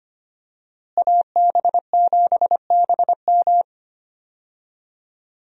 Morse code, heard as a keyed tone, about 25 wpm, 700 Hz. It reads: AB7BM
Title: USA Call Signs - 5 Characters - Repeated 3x - 25wpm